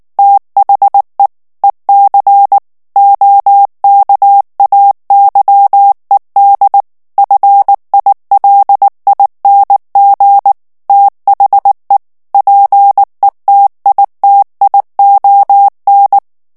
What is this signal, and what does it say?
Morse code, heard as a keyed tone, about 19 wpm, 800 Hz. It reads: THE EC OKAYED FILING THE PETITION
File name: ANGLAIS texte clair 3 vitesse 18 mots minute